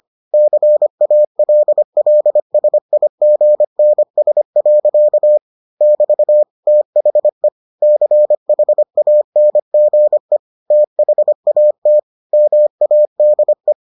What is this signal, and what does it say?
Morse code, heard as a keyed tone, about 25 wpm, 600 Hz. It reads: CALLSIGNS. <BT> THE CHANGE THAT MADE